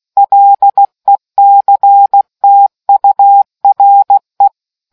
Morse code, heard as a keyed tone, about 16 wpm, 800 Hz. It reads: LECTURE